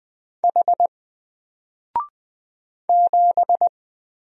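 Morse code, heard as a keyed tone, about 20 wpm, 700 Hz. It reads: H 7
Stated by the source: Single letter and number Morse drill at 20 wpm, from N4LCD.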